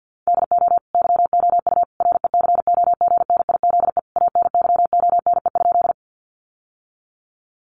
Morse code, heard as a keyed tone, about 50 wpm, 700 Hz. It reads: DO YOU RECOGNIZE ANYONE?
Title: Sentences from Top 1400 Words 50wpm